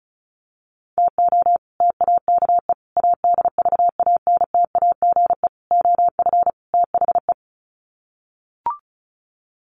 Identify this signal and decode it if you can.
Morse code, heard as a keyed tone, about 34 wpm, 700 Hz. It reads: TO TAKE ADVANTAGE OF THE
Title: Sets of 5 Words 35wpm